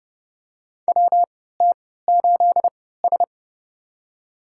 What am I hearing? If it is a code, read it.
Morse code, heard as a keyed tone, about 32 wpm, spaced as for 17 wpm, 700 Hz. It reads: WT8S
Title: USA Call Signs - 4 Characters - 3x Character Spacing - 30wpm